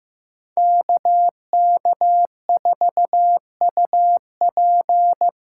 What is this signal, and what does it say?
Morse code, heard as a keyed tone, about 15 wpm, 700 Hz. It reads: KK4UP
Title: USA Call Signs - 6 Characters - 8x Character Spacing - 15wpm